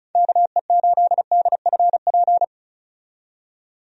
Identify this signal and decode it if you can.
Morse code, heard as a keyed tone, about 36 wpm, 700 Hz. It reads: KE8DFP
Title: USA Call Signs - 6 Characters - 35wpm